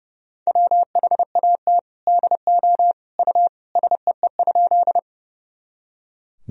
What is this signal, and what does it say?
Morse code, heard as a keyed tone, about 32 wpm, 700 Hz. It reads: WHAT DO U SEE?
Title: Sentences from Top 100 Words and Abbreviations 30wpm